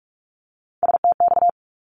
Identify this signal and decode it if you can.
Morse code, heard as a keyed tone, about 44 wpm, 700 Hz. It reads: STX
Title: ARRL Sections 45wpm